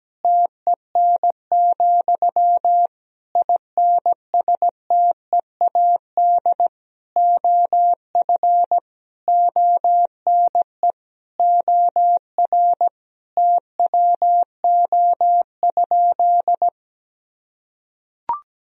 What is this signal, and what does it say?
Morse code, heard as a keyed tone, about 17 wpm, 700 Hz. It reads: TEN, INSTEAD OF ONE OR TWO?
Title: Sentences from Top 1000 Words - Review Part 4 - 17wpm